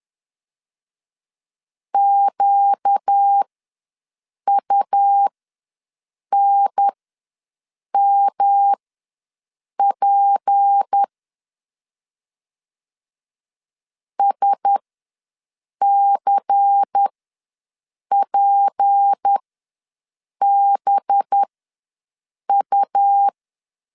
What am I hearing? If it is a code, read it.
Morse code, heard as a keyed tone, about 10 wpm, spaced as for 6 wpm, 800 Hz. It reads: QUNMP SCPBU
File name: LEZIONI / 6 PKX0 / 6 P ws 10wpm 3751